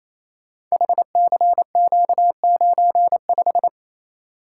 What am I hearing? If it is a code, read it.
Morse code, heard as a keyed tone, about 29 wpm, 700 Hz. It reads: HCQ95
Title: Code Groups - 5 Letter:Numbers - 28wpm